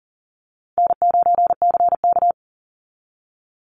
Morse code, heard as a keyed tone, about 40 wpm, 700 Hz. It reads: N9CK
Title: Call Signs Repeated 3x - 40wpm